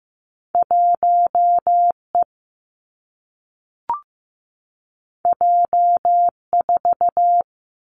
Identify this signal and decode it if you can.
Morse code, heard as a keyed tone, about 15 wpm, 700 Hz. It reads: 1 E J 4
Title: Two Challenging Letter-Number Combinations 15wpm